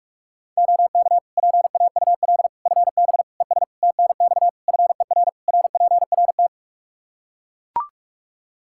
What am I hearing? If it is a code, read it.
Morse code, heard as a keyed tone, about 41 wpm, 700 Hz. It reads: OK PAUL FB ES TNX FER RPRT
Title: Practice QSO - Exchange 2 - Info - 45wpm